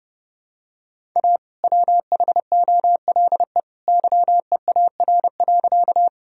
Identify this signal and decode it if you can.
Morse code, heard as a keyed tone, about 30 wpm, 700 Hz. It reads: A WHOLE YEAR.
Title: Sentences from Top 1000 Words - Review Part 1 - 30wpm